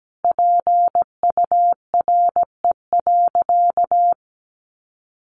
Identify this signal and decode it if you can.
Morse code, heard as a keyed tone, about 17 wpm, 700 Hz. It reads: PURE.
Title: Sentences from Top 1500 Words 17wpm